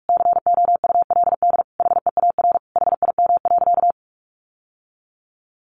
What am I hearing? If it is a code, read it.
Morse code, heard as a keyed tone, about 44 wpm, 700 Hz. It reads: COULD HEAR HIM.
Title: Sentences from Top 1000 Words 45wpm